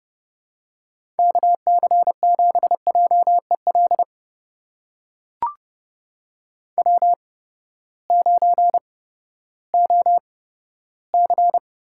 Morse code, heard as a keyed tone, about 30 wpm, spaced as for 15 wpm, 700 Hz. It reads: KC7JEL W9OC
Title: USA Call Signs - 8x Character Spacing - 30wpm